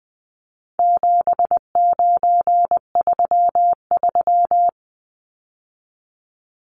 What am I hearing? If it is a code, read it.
Morse code, heard as a keyed tone, about 20 wpm, 700 Hz. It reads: 7933
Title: Numbers - 4-Digit 20wpm